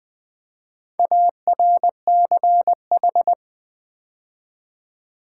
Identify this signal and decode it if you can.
Morse code, heard as a keyed tone, about 20 wpm, 700 Hz. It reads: ARCH
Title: Build A Word - 4 Letters At A Time - 12-Letter Words - 20wpm